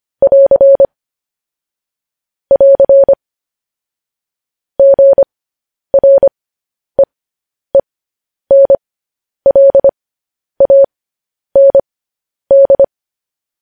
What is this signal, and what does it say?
Morse code, heard as a keyed tone, about 25 wpm, spaced as for 10 wpm, 550 Hz. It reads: <AR> <AR> GREENLAND
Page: news-headlines-in-morse-code-at-10-wpm